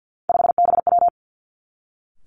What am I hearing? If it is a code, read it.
Morse code, heard as a keyed tone, about 50 wpm, 700 Hz. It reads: 5BW
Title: Three Challenging Letter-Number Combinations 50wpm